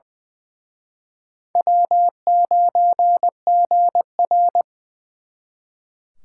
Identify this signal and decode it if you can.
Morse code, heard as a keyed tone, about 24 wpm, 700 Hz. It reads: W9GR
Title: Speed Racing - USA 1x2 Call Signs 20wpm